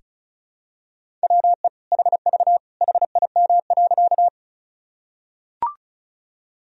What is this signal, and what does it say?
Morse code, heard as a keyed tone, about 36 wpm, 700 Hz. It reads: WE HV HIM.
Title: Sentences from Top 100 Words and Abbreviations 35wpm